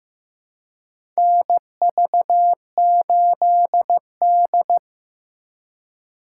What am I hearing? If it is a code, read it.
Morse code, heard as a keyed tone, about 15 wpm, 700 Hz. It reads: NV8D
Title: USA Call Signs - 4x Character Spacing - 15wpm